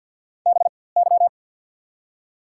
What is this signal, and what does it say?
Morse code, heard as a keyed tone, about 53 wpm, spaced as for 24 wpm, 690 Hz. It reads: BY
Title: 4x Character Spacing - Top 100 Words - 50wpm